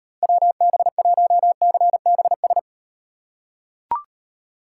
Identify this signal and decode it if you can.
Morse code, heard as a keyed tone, about 38 wpm, 700 Hz. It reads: WB1CBS